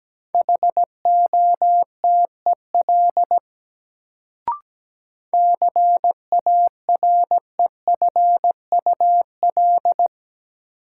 Morse code, heard as a keyed tone, about 17 wpm, 700 Hz. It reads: HOTEL CAREFUL
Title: Sentences from Top 1600 Words - New Word Drill - 17wpm